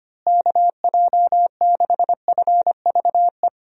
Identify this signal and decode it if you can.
Morse code, heard as a keyed tone, about 25 wpm, 700 Hz. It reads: KJ6FVE
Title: USA Call Signs - 6 Characters - Repeated 3x - 25wpm